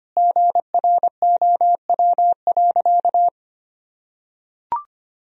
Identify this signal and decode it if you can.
Morse code, heard as a keyed tone, about 25 wpm, 700 Hz. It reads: GROW.
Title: Sentences from Top 1000 Words - Review Part 1 - 25wpm